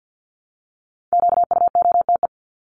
Morse code, heard as a keyed tone, about 50 wpm, 700 Hz. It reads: QUOTE
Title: Top 1751-2000 Words 50wpm